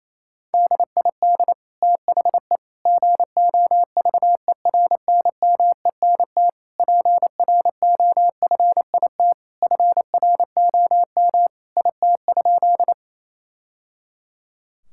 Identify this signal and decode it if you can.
Morse code, heard as a keyed tone, about 27 wpm, 700 Hz. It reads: DID THE GOVERNMENT PROFIT FROM IT?
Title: Sentences from Top 1000 Words - Review Part 5 - 28wpm